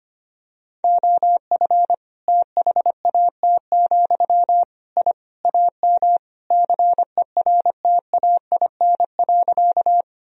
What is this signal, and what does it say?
Morse code, heard as a keyed tone, about 25 wpm, 700 Hz. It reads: OF THAT, I AM CERTAIN.
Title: Sentences from Top 1000 Words - Review Part 2 - 25wpm